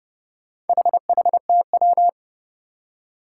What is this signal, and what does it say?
Morse code, heard as a keyed tone, about 29 wpm, 700 Hz. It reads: HHTW
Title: Four Challenging Letter-Number Combinations 30wpm